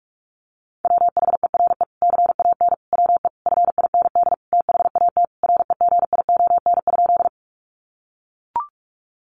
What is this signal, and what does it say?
Morse code, heard as a keyed tone, about 44 wpm, 700 Hz. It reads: WHERE CAN WE FIND THAT REGION?
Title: Sentences from Top 1500 Words 45wpm